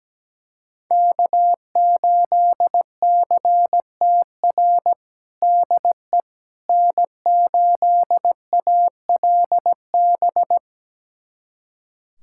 Morse code, heard as a keyed tone, about 17 wpm, 700 Hz. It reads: K8CTR DE N8ALB
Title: USA Callsign DE Callsign - 5 Characters - 17wpm